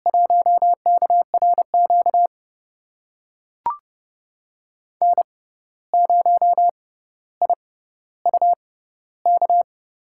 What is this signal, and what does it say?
Morse code, heard as a keyed tone, about 32 wpm, spaced as for 15 wpm, 700 Hz. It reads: 1KRQ N0IUK
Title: USA Call Signs - 5 Characters - 6x Character Spacing - 30wpm